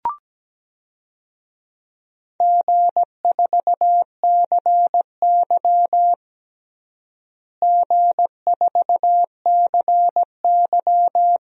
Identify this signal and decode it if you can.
Morse code, heard as a keyed tone, about 17 wpm, 700 Hz. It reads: G4CY G4CY
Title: DXCC-100 Call Signs - 4 Characters - Repeated 2x - 17wpm